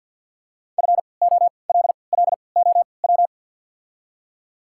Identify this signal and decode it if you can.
Morse code, heard as a keyed tone, about 53 wpm, spaced as for 29 wpm, 690 Hz. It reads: FOLLOW
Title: 3x Character Spacing - Top 300 Words - 50wpm